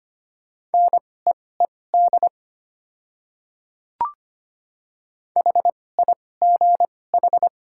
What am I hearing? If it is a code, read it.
Morse code, heard as a keyed tone, about 25 wpm, spaced as for 18 wpm, 700 Hz. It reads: NEED HIGH
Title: 2x Character Spacing - Top 200 Words - 25wpm